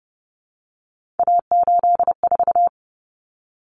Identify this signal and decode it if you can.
Morse code, heard as a keyed tone, about 30 wpm, 700 Hz. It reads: A84
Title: Common Cut Numbers - 3-Digit 30wpm